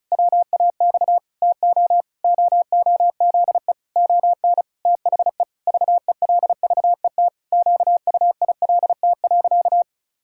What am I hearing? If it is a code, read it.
Morse code, heard as a keyed tone, about 36 wpm, 700 Hz. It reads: WAX TO OOZE ON THE VELVET QUILT.
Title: Pangram Challenge v2 - 35wpm